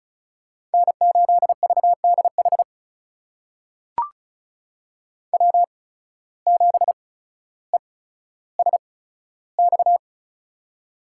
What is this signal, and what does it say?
Morse code, heard as a keyed tone, about 36 wpm, spaced as for 15 wpm, 700 Hz. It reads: N8VDH W7ESX
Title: USA Call Signs - 5 Characters - 8x Character Spacing - 35wpm